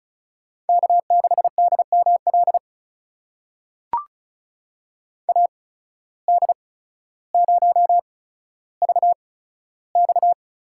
Morse code, heard as a keyed tone, about 36 wpm, spaced as for 15 wpm, 700 Hz. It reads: K6DML AD0VX
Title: USA Call Signs - 5 Characters - 8x Character Spacing - 35wpm